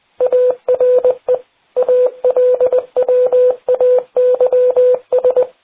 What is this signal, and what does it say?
Morse code, heard as a keyed tone, about 20 wpm, 500 Hz. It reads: ARE ALWAYS